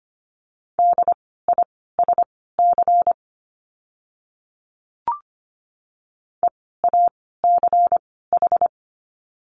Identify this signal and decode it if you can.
Morse code, heard as a keyed tone, about 25 wpm, spaced as for 16 wpm, 700 Hz. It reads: DISC EACH
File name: course / 25:10wpm / 094 Lesson 16f - Letter c - Words 25wpm